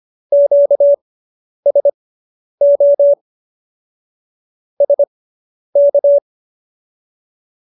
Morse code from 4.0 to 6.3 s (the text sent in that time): SK